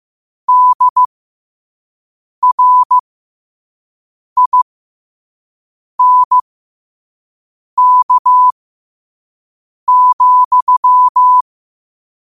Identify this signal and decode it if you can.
Morse code, heard as a keyed tone, about 15 wpm, spaced as for 5 wpm, 1000 Hz. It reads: DRINK,
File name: qotdfiles / 5wpm